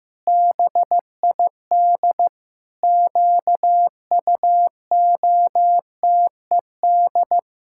Morse code, heard as a keyed tone, about 15 wpm, 700 Hz. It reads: BID QUOTED
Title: Pangram Challenge v2 - 15wpm